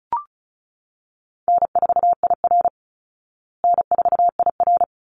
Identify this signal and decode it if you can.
Morse code, heard as a keyed tone, about 34 wpm, 700 Hz. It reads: N4IR N4IR
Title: Call Signs Repeated 3x - 35wpm